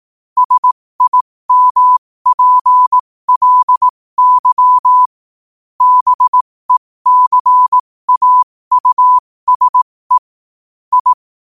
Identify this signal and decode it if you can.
Morse code, heard as a keyed tone, about 18 wpm, spaced as for 15 wpm, 1000 Hz. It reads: SIMPLY BECAUSE I
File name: qotdfiles / 15wpm